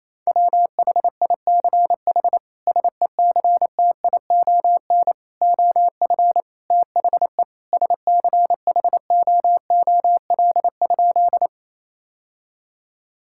Morse code, heard as a keyed tone, about 28 wpm, 700 Hz. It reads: WHICH SECTION OF THE SCHOOL?
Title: Sentences from Top 1000 Words - Review Part 5 - 28wpm